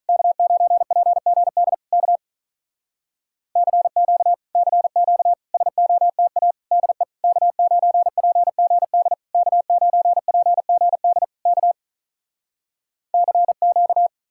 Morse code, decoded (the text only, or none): K9PGD K CQ CQ SOTA DE K9PGD K9PGD K CQ